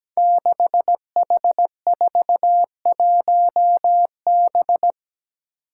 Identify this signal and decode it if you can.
Morse code, heard as a keyed tone, about 17 wpm, 700 Hz. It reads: 6H41B
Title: Five Challenging Letter-Number Combinations 17wpm